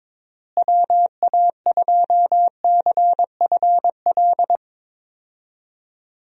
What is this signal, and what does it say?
Morse code, heard as a keyed tone, about 22 wpm, 700 Hz. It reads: WA2CFL
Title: USA Call Signs - 6 Characters - Repeated 3x - 22wpm